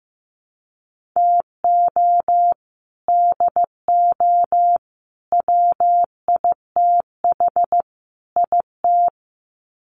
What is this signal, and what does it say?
Morse code, heard as a keyed tone, about 15 wpm, 700 Hz. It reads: TO DO WITH IT
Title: Sets of 4 Words 15wpm